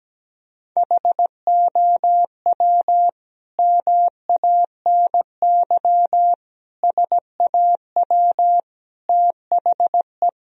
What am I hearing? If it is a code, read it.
Morse code, heard as a keyed tone, about 17 wpm, 700 Hz. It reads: HOW MANY SAW THE
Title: Sentences from Top 1000 Words - Review Part 1 - 17wpm